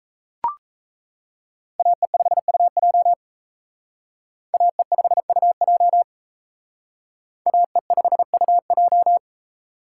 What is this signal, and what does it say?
Morse code, heard as a keyed tone, about 35 wpm, 700 Hz. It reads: AE5UJ AE5UJ AE5UJ
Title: Speed Racing - USA 2x2 Call Signs 28wpm